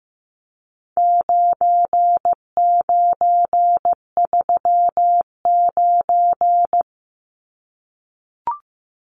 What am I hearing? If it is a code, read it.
Morse code, heard as a keyed tone, about 15 wpm, 700 Hz. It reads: 9939
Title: Numbers - 4-Digit 15wpm